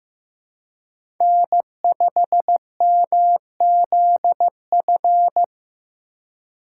Morse code, heard as a keyed tone, about 15 wpm, 700 Hz. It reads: N5MZF
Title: USA Call Signs - 5 Characters - 4x Character Spacing - 15wpm